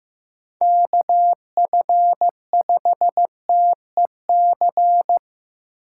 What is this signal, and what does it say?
Morse code, heard as a keyed tone, about 15 wpm, 700 Hz. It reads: KF5TEC
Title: USA Call Signs - 6 Characters - 8x Character Spacing - 15wpm